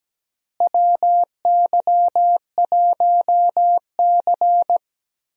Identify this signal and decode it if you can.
Morse code, heard as a keyed tone, about 17 wpm, 700 Hz. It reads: WY1C
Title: USA Call Signs - 4 Characters - Repeated 3x - 17wpm